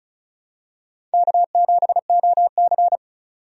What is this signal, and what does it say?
Morse code, heard as a keyed tone, about 34 wpm, 700 Hz. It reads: K7OC
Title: USA Call Signs - 4 Characters - 35wpm